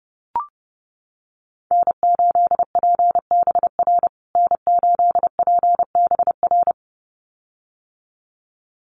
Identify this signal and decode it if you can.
Morse code, heard as a keyed tone, about 30 wpm, 700 Hz. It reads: N8PBR N8PBR
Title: USA Call Signs Repeated 2x 30wpm